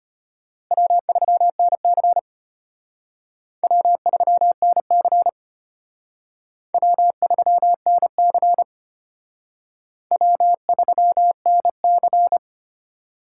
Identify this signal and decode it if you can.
Morse code, heard as a keyed tone, about 32 wpm, 700 Hz. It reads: W3NC W3NC W3NC W3NC